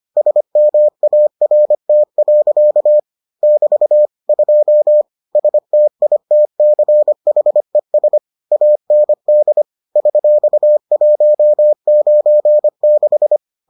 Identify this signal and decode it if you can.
Morse code, heard as a keyed tone, about 25 wpm, 600 Hz. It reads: SMART. <BT> 2 STITCHES AND $196